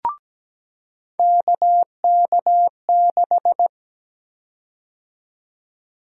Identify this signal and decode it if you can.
Morse code, heard as a keyed tone, about 17 wpm, 700 Hz. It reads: KK6
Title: USA Call Sign Prefixes - 17wpm